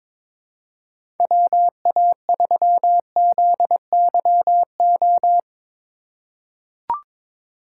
Morse code, heard as a keyed tone, about 22 wpm, 700 Hz. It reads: WA3ZYO
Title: USA Call Signs - 6 Characters - 3x Character Spacing - 22wpm